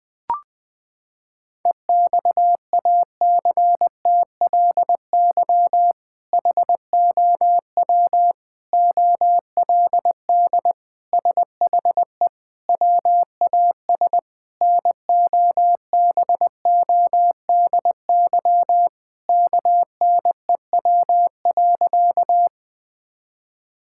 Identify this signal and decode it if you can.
Morse code, heard as a keyed tone, about 20 wpm, 700 Hz. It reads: EXACTLY HOW OLD SHE WAS NOBODY KNEW.
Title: Sentences from Top 1600 Words 20wpm